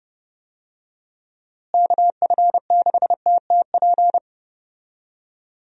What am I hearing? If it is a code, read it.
Morse code, heard as a keyed tone, about 32 wpm, 700 Hz. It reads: KF6TTP
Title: USA Call Signs - 6 Characters - Repeated 3x - 30wpm